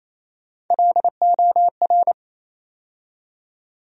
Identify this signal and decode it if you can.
Morse code, heard as a keyed tone, about 28 wpm, 700 Hz. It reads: LOR
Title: USA Call Sign Postfixes - 3 Letters 28wpm